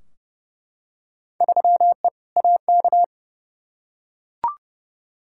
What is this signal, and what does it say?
Morse code, heard as a keyed tone, about 30 wpm, 700 Hz. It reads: 3E AK
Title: Field Day Exchanges Encore - 30wpm